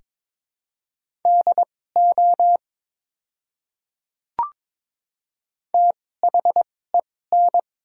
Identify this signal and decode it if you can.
Morse code, heard as a keyed tone, about 22 wpm, spaced as for 16 wpm, 700 Hz. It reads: DO THEN
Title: 2x Character Spacing - Top 200 Words - 22wpm